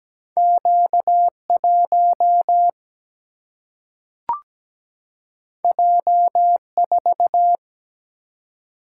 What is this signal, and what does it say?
Morse code, heard as a keyed tone, about 17 wpm, 700 Hz. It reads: Q1 J4
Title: Single-Letter Single-Number 17wpm